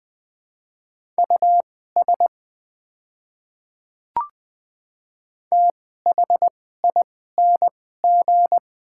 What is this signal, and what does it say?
Morse code, heard as a keyed tone, about 20 wpm, spaced as for 14 wpm, 700 Hz. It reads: US THING
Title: 2x Character Spacing - Top 200 Words - 20wpm